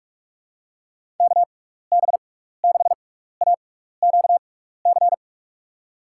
Morse code, heard as a keyed tone, about 46 wpm, spaced as for 15 wpm, 700 Hz. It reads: KB6AQC
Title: USA Call Signs - 6 Characters - 6x Character Spacing - 45wpm